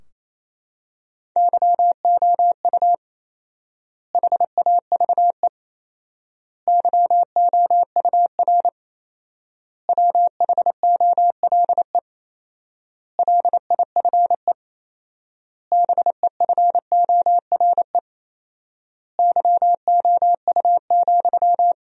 Morse code, heard as a keyed tone, about 28 wpm, 700 Hz. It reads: YOU HAVE YOUR WHOLE LIFE BEFORE YOU,
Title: 4x Word-Spacing - Sentences from Top 400 Words - 28wpm